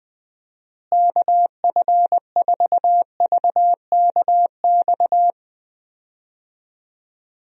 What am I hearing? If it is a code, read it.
Morse code, heard as a keyed tone, about 21 wpm, 700 Hz. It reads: KF4VKX
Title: USA Call Signs - 6 Characters - Repeated 2x - 20wpm